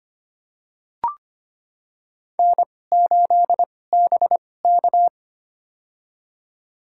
Morse code, heard as a keyed tone, about 25 wpm, spaced as for 18 wpm, 700 Hz. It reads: N8BK
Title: USA Call Signs - 4 Characters - 2x Character Spacing - 25wpm